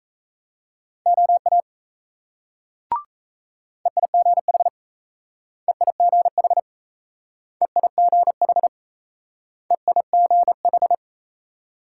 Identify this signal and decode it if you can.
Morse code, heard as a keyed tone, about 38 wpm, 700 Hz. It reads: OA EIGH EIGH EIGH EIGH